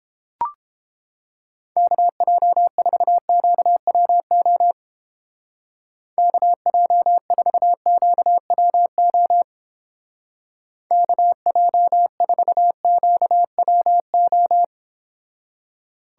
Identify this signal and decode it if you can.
Morse code, heard as a keyed tone, about 29 wpm, 700 Hz. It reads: KJ4QWO KJ4QWO KJ4QWO